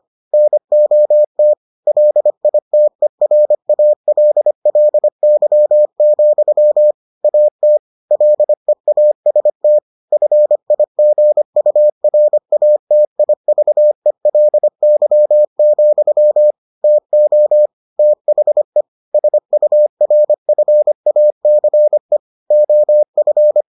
Morse code, heard as a keyed tone, about 25 wpm, 600 Hz. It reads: NOT LITERALLY, AT LEAST FIGURATIVELY, TO THE SURFACE OF